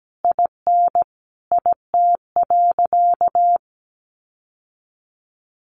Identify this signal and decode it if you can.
Morse code, heard as a keyed tone, about 17 wpm, 700 Hz. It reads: IN IT.
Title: Sentences from Top 1100 Words 17wpm